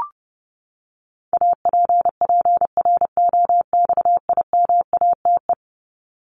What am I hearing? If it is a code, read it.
Morse code, heard as a keyed tone, about 30 wpm, 700 Hz. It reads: APPROXIMATE
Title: LY Suffix - Top 100 Words with Base Words 30wpm